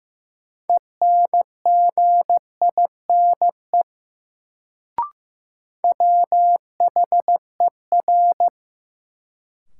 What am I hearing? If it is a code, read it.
Morse code, heard as a keyed tone, about 15 wpm, 700 Hz. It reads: ENGINE WHER
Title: Sentences from Top 1500 Words - New Word Drill Alt - 15wpm